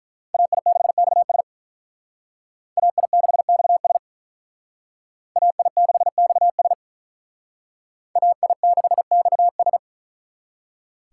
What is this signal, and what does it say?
Morse code, heard as a keyed tone, about 42 wpm, 700 Hz. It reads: AI6XS AI6XS AI6XS AI6XS
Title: Speed Racing - USA 2x2 Call Signs 35wpm